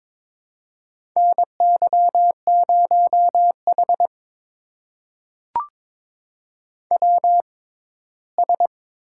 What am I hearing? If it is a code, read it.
Morse code, heard as a keyed tone, about 23 wpm, spaced as for 15 wpm, 700 Hz. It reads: NY0H WS
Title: USA Call Signs - 4 Characters - 6x Character Spacing - 22wpm